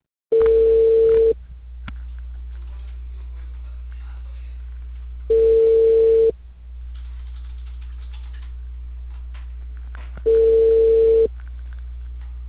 Ringback tone (China, 450 Hz).